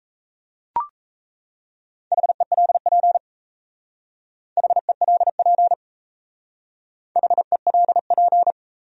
Morse code, heard as a keyed tone, about 34 wpm, 700 Hz. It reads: HELP HELP HELP
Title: Speed-Racing Top 300 Words 28wpm